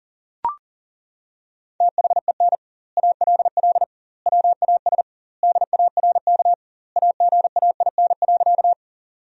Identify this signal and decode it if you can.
Morse code, heard as a keyed tone, about 37 wpm, 700 Hz. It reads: THEN ALL WAS DARK AGAIN.
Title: 2x Word-Spacing - Sentences from Top 500 Words - 40wpm